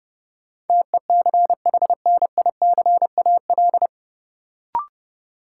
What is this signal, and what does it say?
Morse code, heard as a keyed tone, about 30 wpm, 700 Hz. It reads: TECHNICAL
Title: Sentences from Top 1100 Words - New Word Drill - 30wpm